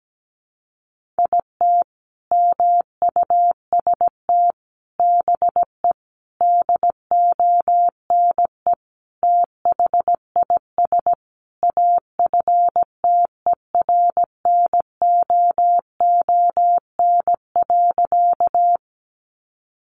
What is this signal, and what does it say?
Morse code, heard as a keyed tone, about 17 wpm, 700 Hz. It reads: IT MUST BE DONE THIS AFTERNOON.
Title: Sentences from Top 1500 Words 17wpm